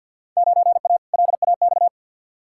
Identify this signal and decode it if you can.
Morse code, heard as a keyed tone, about 48 wpm, 700 Hz. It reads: 9A LAX
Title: Field Day Exchanges Encore - 50wpm